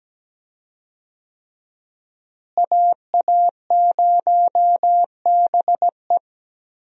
Morse code, heard as a keyed tone, about 17 wpm, 700 Hz. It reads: AA0BE
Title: USA Call Signs - 5 Characters - Repeated 3x - 17wpm